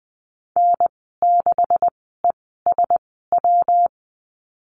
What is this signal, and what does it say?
Morse code, heard as a keyed tone, about 20 wpm, spaced as for 15 wpm, 700 Hz. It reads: N6ESW
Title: 128 Lesson 21g - Number 6 - Call Signs 20wpm